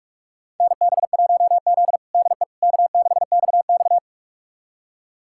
Morse code, heard as a keyed tone, about 46 wpm, 700 Hz. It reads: NB1Z DE K6XX
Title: USA Callsign DE Callsign - 4 Characters - 45wpm